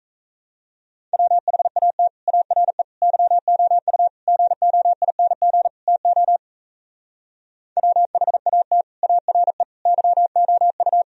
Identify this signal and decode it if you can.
Morse code, heard as a keyed tone, about 35 wpm, 700 Hz. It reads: WHAT ARE YOU GOING TO WHAT ARE YOU